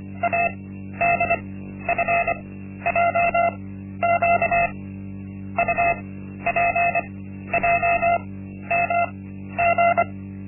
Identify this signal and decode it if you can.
Morse code, heard as a keyed tone, about 25 wpm, spaced as for 12 wpm, 690 Hz. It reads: ADFJQ UPJMG
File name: Linksmakalnis-1